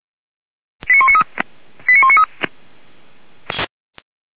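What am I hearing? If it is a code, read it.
Five-tone selective call (ZVEI), heard as quick runs of five tones, tones of about 70 ms. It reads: 89193 89193
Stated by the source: Received on SDR.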